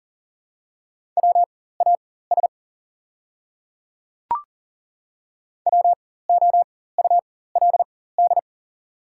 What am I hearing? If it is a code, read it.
Morse code, heard as a keyed tone, about 37 wpm, spaced as for 18 wpm, 700 Hz. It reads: WAS WOULD
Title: 4x Character Spacing - Top 300 Words - 40wpm